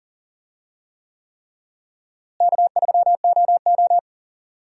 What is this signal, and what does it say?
Morse code, heard as a keyed tone, about 40 wpm, 700 Hz. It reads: K3OO
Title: USA Call Signs - 4 Characters - Repeated 3x - 40wpm